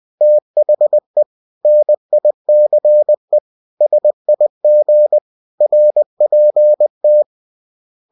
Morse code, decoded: THE NICE SIG RPT

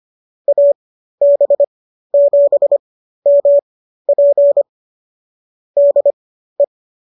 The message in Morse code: AB7MP DE

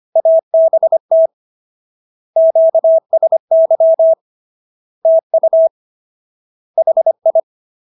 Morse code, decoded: ABT QSY TU HI

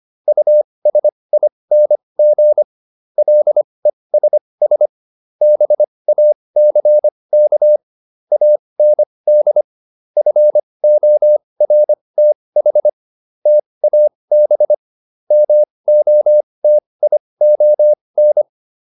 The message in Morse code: USING LESS BACK AND FORTH TAB MOTION

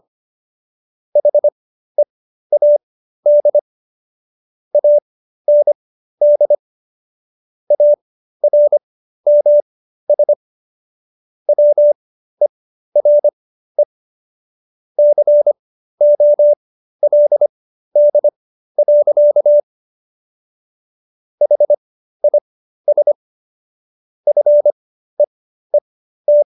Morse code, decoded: HEAD AND ARMS WERE COLD. HIS FEET

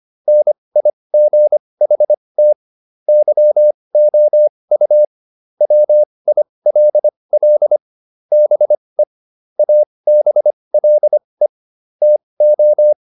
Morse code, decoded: NIGHT YOU WILL BE ABLE TO

